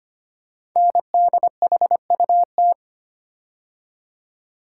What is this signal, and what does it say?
Morse code, heard as a keyed tone, about 25 wpm, 700 Hz. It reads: NDHUT
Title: Code Groups - 5 Letters - 25wpm